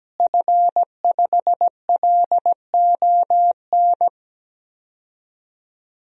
Morse code, heard as a keyed tone, about 17 wpm, 700 Hz. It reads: F5LON
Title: DXCC-100 Call Signs - 5 Characters - Repeated 2x - 17wpm